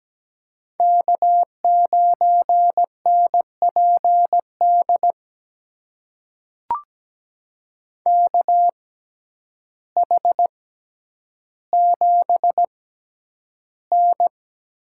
Morse code, heard as a keyed tone, about 17 wpm, spaced as for 9 wpm, 700 Hz. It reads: K9NPD KH7N
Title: USA Call Signs - 5 Characters - 6x Character Spacing - 17wpm